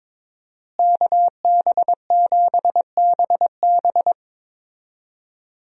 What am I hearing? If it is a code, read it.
Morse code, heard as a keyed tone, about 21 wpm, 700 Hz. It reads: KB7BB
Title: USA Call Signs - 5 Characters - Repeated 2x - 22wpm